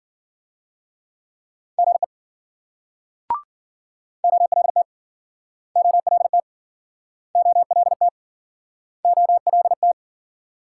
Morse code, heard as a keyed tone, about 56 wpm, 700 Hz. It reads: GE OLT OLT OLT OLT